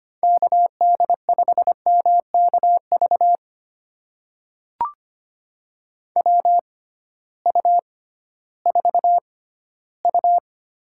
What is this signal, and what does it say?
Morse code, heard as a keyed tone, about 25 wpm, spaced as for 15 wpm, 700 Hz. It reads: KD5MKV WU4U
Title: USA Call Signs - 6x Character Spacing - 25wpm